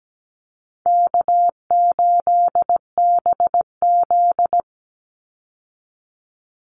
Morse code, heard as a keyed tone, about 17 wpm, 700 Hz. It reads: K8BZ
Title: Easy Call Signs Repeated 3x - 17wpm